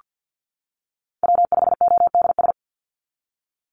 Morse code, heard as a keyed tone, about 48 wpm, 700 Hz. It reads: W5ODS
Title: NA SOTA Activator Callsigns 50wpm